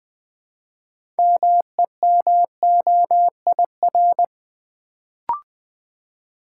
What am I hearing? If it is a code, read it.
Morse code, heard as a keyed tone, about 20 wpm, 700 Hz. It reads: MEMOIR
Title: Build A Word - 3 Letters At A Time - 6-Letter Words - 20wpm